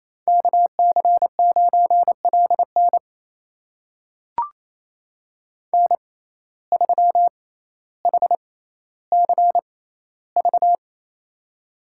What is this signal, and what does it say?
Morse code, heard as a keyed tone, about 28 wpm, spaced as for 14 wpm, 700 Hz. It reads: KC9LN N3HCV
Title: USA Call Signs - 5 Characters - 6x Character Spacing - 28wpm